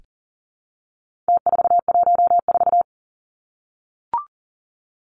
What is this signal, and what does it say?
Morse code, heard as a keyed tone, about 40 wpm, 700 Hz. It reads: T414
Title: Numbers - Contest Style - 4-Digit 40wpm